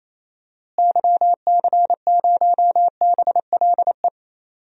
Morse code, heard as a keyed tone, about 28 wpm, 700 Hz. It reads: YC0BLE
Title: DXCC-100 Call Signs - 6 Characters - Repeated 3x - 28wpm